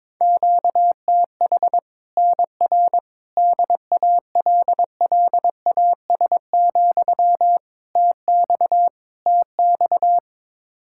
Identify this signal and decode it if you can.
Morse code, heard as a keyed tone, about 21 wpm, 700 Hz. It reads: QTH NR DALLAS, TX TX
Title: Practice QSO - Exchange 1 - Beginning - 22wpm